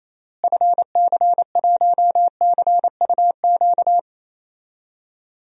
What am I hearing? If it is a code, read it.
Morse code, heard as a keyed tone, about 28 wpm, 700 Hz. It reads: FC1CUQ
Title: DXCC-100 Call Signs - 6 Characters - Repeated 3x - 28wpm